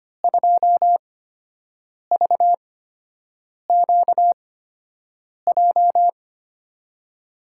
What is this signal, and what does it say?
Morse code, heard as a keyed tone, about 25 wpm, spaced as for 7 wpm, 700 Hz. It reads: 2VQJ
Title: USA Call Signs - 8x Character Spacing - 25wpm